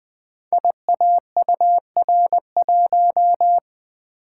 Morse code, heard as a keyed tone, about 20 wpm, 700 Hz. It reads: IAUR1
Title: Code Groups - 5 Letter:Numbers - 20wpm